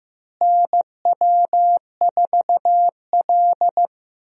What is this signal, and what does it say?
Morse code, heard as a keyed tone, about 15 wpm, 700 Hz. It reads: NW4L
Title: USA Call Signs - 4 Characters - Repeated 3x - 15wpm